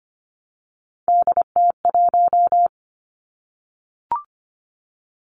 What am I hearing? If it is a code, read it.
Morse code, heard as a keyed tone, about 25 wpm, 700 Hz. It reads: DT1
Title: Three Challenging Letter-Number Combinations 25wpm